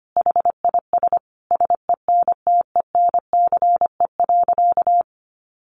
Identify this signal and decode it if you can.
Morse code, heard as a keyed tone, about 25 wpm, 700 Hz. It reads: HIS SENTENCE.
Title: Sentences from Top 1200 Words 25wpm